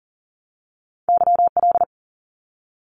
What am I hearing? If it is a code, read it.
Morse code, heard as a keyed tone, about 40 wpm, 700 Hz. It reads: YL
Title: Abbreviations 40wpm